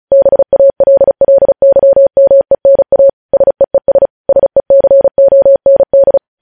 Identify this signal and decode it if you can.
Morse code, heard as a keyed tone, about 36 wpm, 550 Hz. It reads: BALLYMENA SEES SECOND
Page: news headlines in morse code at 35 wpm